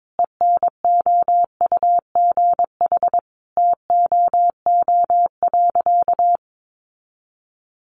Morse code, heard as a keyed tone, about 22 wpm, 700 Hz. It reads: ENOUGH TOO.
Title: Additional Easy Phrases 22wpm